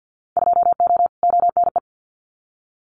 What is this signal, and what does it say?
Morse code, heard as a keyed tone, about 50 wpm, 700 Hz. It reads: 2O ONE
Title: Winter Field Day Exchanges Encore 50wpm